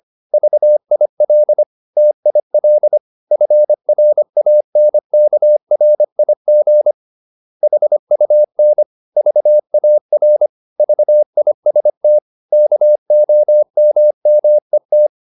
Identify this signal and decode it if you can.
Morse code, heard as a keyed tone, about 25 wpm, 600 Hz. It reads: VIL TIL FRANKRIG HUN VAR VIST KOMMET
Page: Ferie 25 wpm